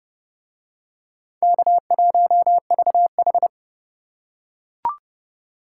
Morse code, heard as a keyed tone, about 28 wpm, 700 Hz. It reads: K1VH